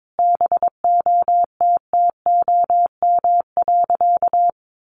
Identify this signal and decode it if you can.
Morse code, heard as a keyed tone, about 22 wpm, 700 Hz. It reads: BOTTOM.
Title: Sentences from Top 1400 Words 22wpm